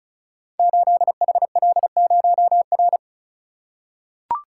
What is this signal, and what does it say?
Morse code, heard as a keyed tone, about 36 wpm, 690 Hz. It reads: 8HL0R
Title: Code Groups - 5 Letter:Numbers - 35wpm